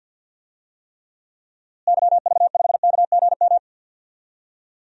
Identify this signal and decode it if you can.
Morse code, heard as a keyed tone, about 50 wpm, 700 Hz. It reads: YV5KGM
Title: DXCC-100 Call Signs - 6 Characters - Repeated 3x - 50wpm